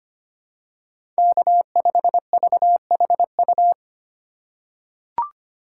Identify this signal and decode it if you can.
Morse code, heard as a keyed tone, about 25 wpm, 700 Hz. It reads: K5VHU
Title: USA Call Signs - 5 Characters - 4x Character Spacing - 25wpm